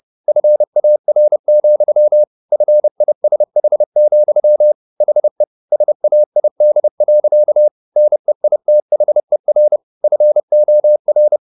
Morse code, heard as a keyed tone, about 30 wpm, 600 Hz. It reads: FAR, FISH, HE SAID. NEITHER FOR